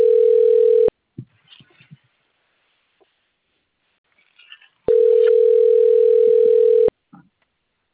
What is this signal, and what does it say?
Ringback tone (North America), 440 + 480 Hz.